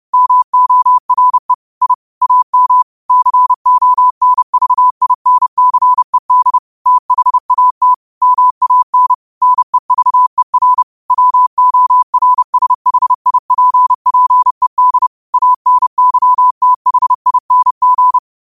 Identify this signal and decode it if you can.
Morse code, heard as a keyed tone, about 30 wpm, 1000 Hz. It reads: MORE I AM CONVINCED THAT MAN NEVER WORSHIPPED ANYTHING
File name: qotdfiles / 30wpm